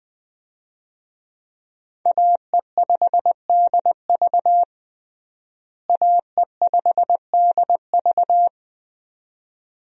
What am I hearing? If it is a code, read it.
Morse code, heard as a keyed tone, about 20 wpm, 700 Hz. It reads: AE5DV AE5DV
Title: USA Call Signs - 5 Characters - Repeated 2x - 20wpm